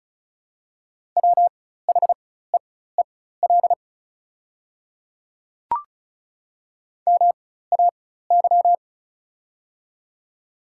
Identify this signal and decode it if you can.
Morse code, heard as a keyed tone, about 35 wpm, spaced as for 16 wpm, 700 Hz. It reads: WHEEL MAY
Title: 4x Character Spacing - Top 500 Words - 35wpm